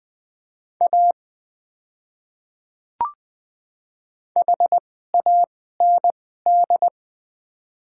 Morse code, heard as a keyed tone, about 21 wpm, spaced as for 15 wpm, 700 Hz. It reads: A HAND
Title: 2x Character Spacing - Top 500 Words - 20wpm